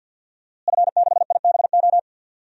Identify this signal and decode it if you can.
Morse code, heard as a keyed tone, about 46 wpm, 700 Hz. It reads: F6IBO
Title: DXCC-100 Call Signs - 5 Characters - Repeated 2x - 50wpm